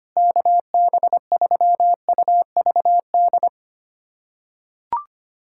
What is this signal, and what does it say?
Morse code, heard as a keyed tone, about 25 wpm, 700 Hz. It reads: KB3UVD
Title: USA Callsign DE Callsign 25wpm